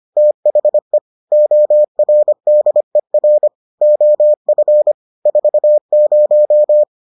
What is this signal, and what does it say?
Morse code, heard as a keyed tone, about 25 wpm, 600 Hz. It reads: THE ORDER OF 40